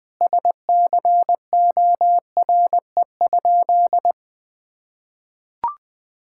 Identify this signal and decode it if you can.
Morse code, heard as a keyed tone, about 21 wpm, 700 Hz. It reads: SCORE?
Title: Sentences from Top 1600 Words 20wpm